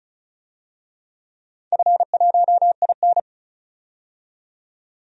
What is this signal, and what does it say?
Morse code, heard as a keyed tone, about 34 wpm, 700 Hz. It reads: F1IN
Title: DXCC-100 Call Signs - 4 Characters - 35wpm